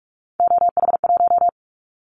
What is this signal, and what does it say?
Morse code, heard as a keyed tone, about 44 wpm, 700 Hz. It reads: OH1
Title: Callsign Prefixes for Top 25 DXCC - Repeated 3x - 45wpm